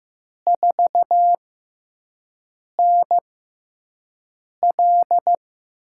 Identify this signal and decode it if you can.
Morse code, heard as a keyed tone, about 15 wpm, spaced as for 5 wpm, 700 Hz. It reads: 4NL
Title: USA Call Signs - 4 Characters - 6x Character Spacing - 15wpm